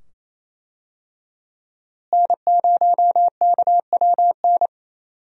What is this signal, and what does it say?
Morse code, heard as a keyed tone, about 27 wpm, 700 Hz. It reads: N0KWN
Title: USA Call Signs - 5 Characters - Repeated 2x - 28wpm